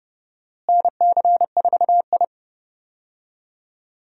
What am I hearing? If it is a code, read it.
Morse code, heard as a keyed tone, about 30 wpm, 700 Hz. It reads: NC4I
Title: USA Call Signs - 4 Characters - Repeated 2x - 30wpm